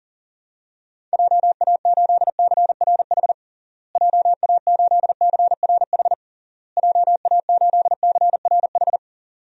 Morse code, heard as a keyed tone, about 40 wpm, 700 Hz. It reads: JA8CRH JA8CRH JA8CRH
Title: DXCC-100 Call Signs - 6 Characters - Repeated 3x - 40wpm